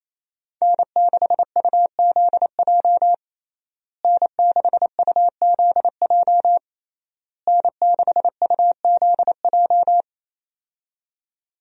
Morse code, heard as a keyed tone, about 27 wpm, 700 Hz. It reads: N6UZJ N6UZJ N6UZJ
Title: USA Call Signs - 5 Characters - Repeated 3x - 28wpm